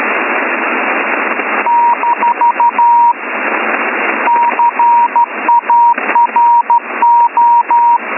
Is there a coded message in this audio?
Yes: Morse code from 1.7 to 8.0 s, heard as a keyed tone, about 13 wpm, 950 Hz, with radio noise. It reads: - CARO